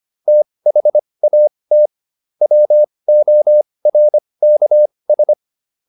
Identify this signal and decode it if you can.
Morse code, heard as a keyed tone, about 25 wpm, spaced as for 20 wpm, 600 Hz. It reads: THAT WORKS